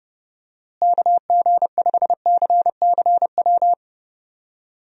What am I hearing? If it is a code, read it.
Morse code, heard as a keyed tone, about 30 wpm, 700 Hz. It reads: KG5CCW
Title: USA Call Signs - 6 Characters - Repeated 3x - 30wpm